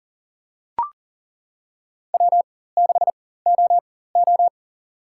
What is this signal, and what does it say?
Morse code, heard as a keyed tone, about 40 wpm, spaced as for 19 wpm, 700 Hz. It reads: W6OO
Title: USA Call Signs - 4 Characters - 4x Character Spacing - 40wpm